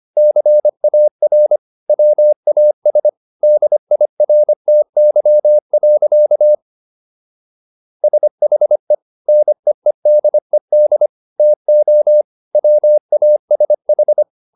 Morse code, heard as a keyed tone, about 25 wpm, 600 Hz. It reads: CAR WAS DIRTY. SHE NEEDED TO WASH